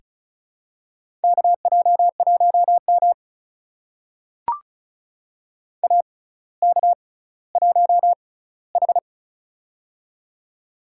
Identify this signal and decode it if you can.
Morse code, heard as a keyed tone, about 36 wpm, spaced as for 18 wpm, 700 Hz. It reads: KJ1M AK1H
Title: USA Call Signs - 4 Characters - 6x Character Spacing - 35wpm